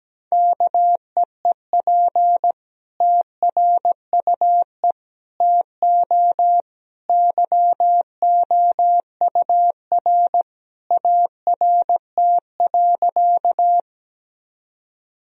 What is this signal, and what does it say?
Morse code, heard as a keyed tone, about 17 wpm, 700 Hz. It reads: KEEP TRUE TO YOUR ART.
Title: Sentences from Top 1000 Words - Review Part 3 - 17wpm